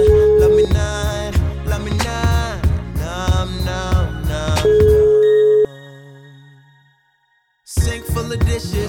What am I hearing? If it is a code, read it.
Ringback tone (Europe, 425 Hz).